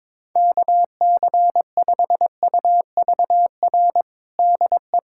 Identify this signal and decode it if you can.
Morse code, heard as a keyed tone, about 22 wpm, 700 Hz. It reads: KC5UVR DE
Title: USA Callsign DE Callsign 22wpm